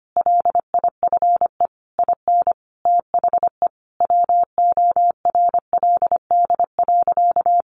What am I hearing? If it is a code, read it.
Morse code, heard as a keyed tone, about 25 wpm, 700 Hz. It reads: LIFE IN THE WORLD.
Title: Sentences from Top 1100 Words 25wpm